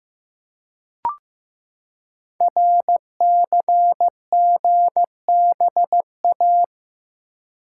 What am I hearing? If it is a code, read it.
Morse code, heard as a keyed tone, about 15 wpm, 700 Hz. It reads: RCGBA
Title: Code Groups - 5 Letters - 15wpm